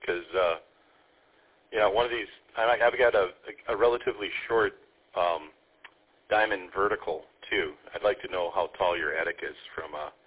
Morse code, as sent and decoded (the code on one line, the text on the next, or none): none